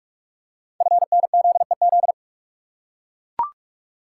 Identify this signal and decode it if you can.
Morse code, heard as a keyed tone, about 44 wpm, 700 Hz. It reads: FNZEZ
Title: Code Groups - 5 Letters - 45wpm